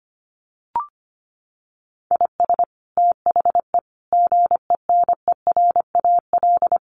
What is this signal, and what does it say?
Morse code, heard as a keyed tone, about 25 wpm, 700 Hz. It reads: IS THE GENERAL
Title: Sentences from Top 600 Words 25wpm